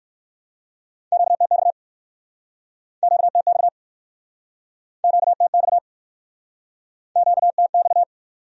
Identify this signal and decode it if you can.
Morse code, heard as a keyed tone, about 61 wpm, 700 Hz. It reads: QTX QTX QTX QTX